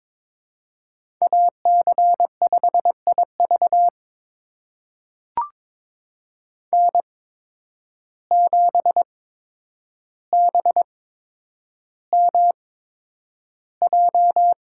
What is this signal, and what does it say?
Morse code, heard as a keyed tone, about 22 wpm, spaced as for 9 wpm, 700 Hz. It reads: AC5IV N7BMJ
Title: USA Call Signs - 5 Characters - 8x Character Spacing - 22wpm